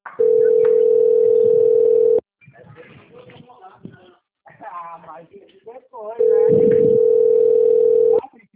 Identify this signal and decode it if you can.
Ringback tone (North America), 440 + 480 Hz, with speech.
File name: out- - - - - .8638